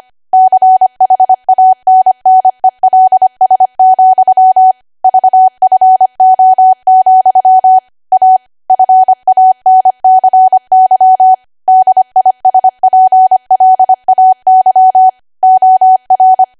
Morse code, heard as a keyed tone, about 25 wpm, 750 Hz. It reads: CHANNELS, VFO, A FANCY DISPLAY OR
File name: cwa / W1AW-Level3 / 25WPM